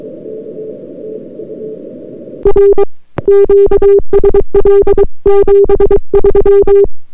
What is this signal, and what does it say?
Morse code, heard as a keyed tone, about 23 wpm, 400 Hz. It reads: R QSL 73